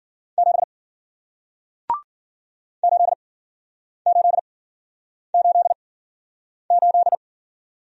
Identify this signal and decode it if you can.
Morse code, heard as a keyed tone, about 54 wpm, spaced as for 9 wpm, 700 Hz. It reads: 7 8888